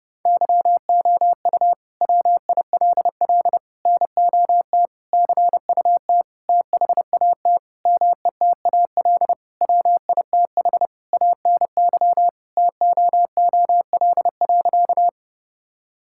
Morse code, heard as a keyed tone, about 30 wpm, 700 Hz. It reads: YOU WILL NOT CUT THAT METAL WITH ANY TOOL.
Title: Sentences from Top 1000 Words - Review Part 6 - 30wpm